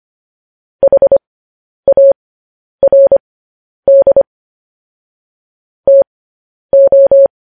Morse code, heard as a keyed tone, about 25 wpm, spaced as for 10 wpm, 550 Hz. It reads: HARD TO